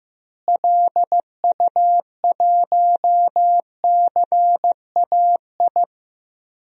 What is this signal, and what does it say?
Morse code, heard as a keyed tone, about 15 wpm, 700 Hz. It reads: LU1CAI
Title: DXCC-100 Call Signs - 6 Characters - Repeated 3x - 15wpm